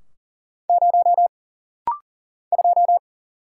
Morse code, heard as a keyed tone, about 40 wpm, 700 Hz. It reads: 0 2
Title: Single Letter-Number Alt - Rapid-Fire 40wpm